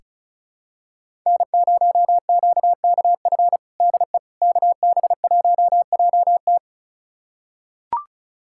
Morse code, heard as a keyed tone, about 36 wpm, 700 Hz. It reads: N0QKF DE KB1JT
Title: USA Callsign DE Callsign - 5 Characters - 35wpm